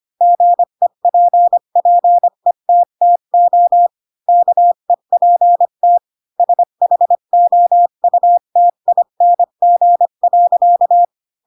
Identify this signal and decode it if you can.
Morse code, heard as a keyed tone, about 25 wpm, 700 Hz. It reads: GEPPETTO KEPT SHOUTING.
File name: Pinocchio / Pinocchio ch3 20 25 ik1hge